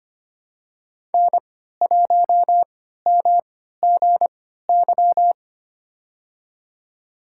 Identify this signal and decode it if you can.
Morse code, heard as a keyed tone, about 25 wpm, spaced as for 14 wpm, 700 Hz. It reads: N1MGY
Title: USA Call Signs - 3x Character Spacing - 25wpm